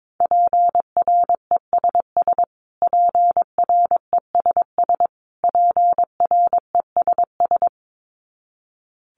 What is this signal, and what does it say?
Morse code, heard as a keyed tone, about 22 wpm, 700 Hz. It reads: PRESS PRESS PRESS
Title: Top 500 Words Repeated 22wpm